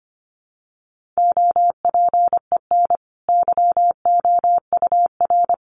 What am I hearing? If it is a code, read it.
Morse code, heard as a keyed tone, about 25 wpm, 700 Hz. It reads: OPEN YOUR
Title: Sentences from Top 1500 Words 25wpm